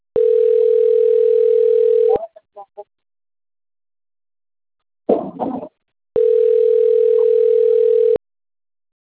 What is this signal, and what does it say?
Ringback tone (North America), 440 + 480 Hz.
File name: rg-1600-unknown- - - .75811